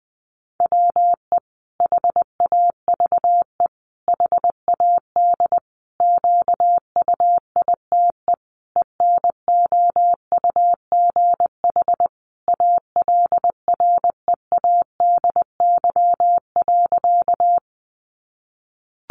Morse code, heard as a keyed tone, about 20 wpm, 700 Hz. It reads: WE HAVE HAD QUITE ENOUGH ALREADY.
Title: Sentences from Top 700 Words 20wpm